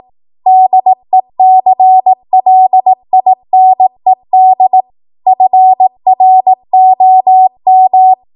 Morse code, heard as a keyed tone, about 18 wpm, 750 Hz. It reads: DECLINED FROM
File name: Archive / 18 WPM / 18WPM